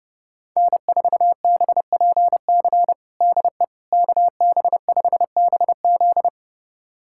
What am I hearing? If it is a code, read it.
Morse code, heard as a keyed tone, about 30 wpm, 700 Hz. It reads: N4BPC DE KB5BZ
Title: USA Callsign DE Callsign - 5 Characters - 30wpm